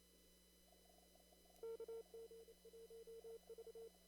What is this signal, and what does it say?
Morse code, heard as a keyed tone, about 31 wpm, 460 Hz. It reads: KG1V
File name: CW / kt5j wpx cw-op-k5pi